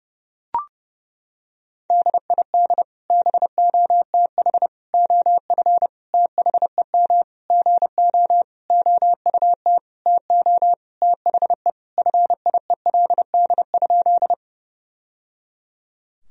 Morse code, heard as a keyed tone, about 30 wpm, 700 Hz. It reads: DID BOTH OF THEM GO OUT TO THE FIELD?
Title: Sentences from Top 1000 Words - Review Part 2 - 30wpm